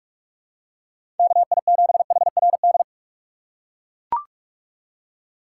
Morse code, heard as a keyed tone, about 44 wpm, 700 Hz. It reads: KI7HRD
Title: USA Call Signs - 6 Characters - 6x Character Spacing - 45wpm